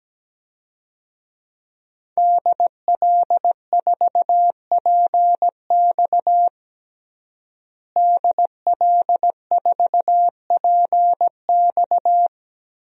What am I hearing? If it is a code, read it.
Morse code, heard as a keyed tone, about 17 wpm, 700 Hz. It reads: DL4PX DL4PX
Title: DXCC-100 Call Signs - 5 Characters - Repeated 3x - 17wpm